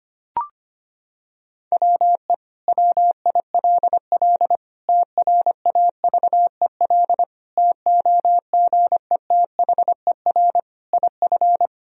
Morse code, heard as a keyed tone, about 25 wpm, 700 Hz. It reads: WE WILL TRAVEL TOGETHER IF
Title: Sentences from Top 1000 Words - Review Part 2 - 25wpm